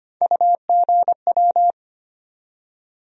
Morse code, heard as a keyed tone, about 25 wpm, 700 Hz. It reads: UGW
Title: USA Call Sign Postfixes - 3 Letters 25wpm